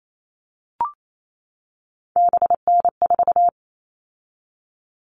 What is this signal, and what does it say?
Morse code, heard as a keyed tone, about 28 wpm, 700 Hz. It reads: BN4
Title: All Cut Numbers - 3-Digit 28wpm